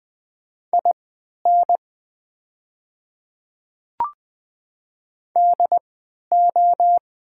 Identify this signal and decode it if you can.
Morse code, heard as a keyed tone, about 20 wpm, spaced as for 11 wpm, 700 Hz. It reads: IN DO
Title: 3x Character Spacing - Top 500 Words - 20wpm